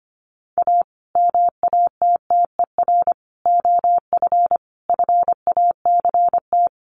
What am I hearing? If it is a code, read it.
Morse code, heard as a keyed tone, about 25 wpm, 700 Hz. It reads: A MATTER OF FACT
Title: Sets of 4 Words 25wpm